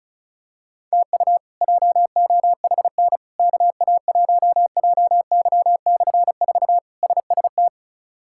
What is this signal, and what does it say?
Morse code, heard as a keyed tone, about 34 wpm, 700 Hz. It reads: TU JOHN KA1JY/4 SST
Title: K1USN SST QSO Practice - 35wpm